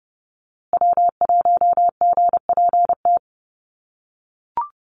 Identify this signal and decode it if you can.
Morse code, heard as a keyed tone, about 30 wpm, 700 Hz. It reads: W1GPT
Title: USA Call Signs Repeated 3x 30wpm